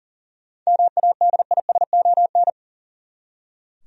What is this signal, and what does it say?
Morse code, heard as a keyed tone, about 40 wpm, 700 Hz. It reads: MADISON
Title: USA Common Names - 40wpm